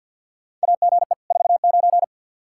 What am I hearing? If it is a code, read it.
Morse code, heard as a keyed tone, about 51 wpm, 700 Hz. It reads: AGE 49
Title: Practice QSO - Exchange 3 - More Info - 50wpm